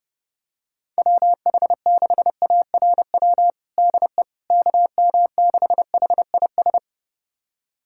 Morse code, heard as a keyed tone, about 30 wpm, 700 Hz. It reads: WH6ARW DE KM6HIS
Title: USA Callsign DE Callsign - 6 Characters - 30wpm